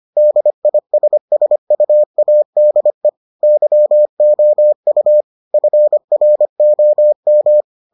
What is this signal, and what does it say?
Morse code, heard as a keyed tone, about 25 wpm, 600 Hz. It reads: DISSUADE YOU FROM